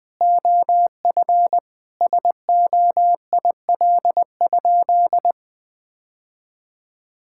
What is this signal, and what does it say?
Morse code, heard as a keyed tone, about 20 wpm, 700 Hz. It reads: OF SOIL?
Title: Sentences from Top 1600 Words 20wpm